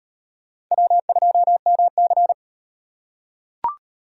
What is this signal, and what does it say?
Morse code, heard as a keyed tone, about 38 wpm, 700 Hz. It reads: W2MC